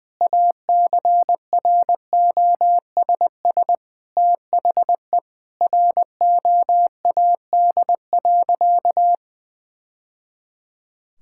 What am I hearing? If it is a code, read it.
Morse code, heard as a keyed tone, about 20 wpm, 700 Hz. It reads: ACROSS THE ROAD.
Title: Sentences from Top 1000 Words - Review Part 5 - 20wpm